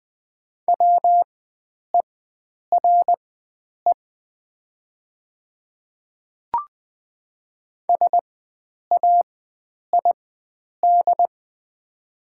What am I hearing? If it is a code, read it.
Morse code, heard as a keyed tone, about 20 wpm, spaced as for 9 wpm, 700 Hz. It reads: WERE SAID
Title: 4x Character Spacing - Top 100 Words - 20wpm